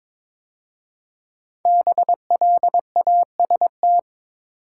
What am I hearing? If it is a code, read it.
Morse code, heard as a keyed tone, about 22 wpm, 700 Hz. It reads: BLAST
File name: 5 Letter Words - First 3 Letters Given - 22wpm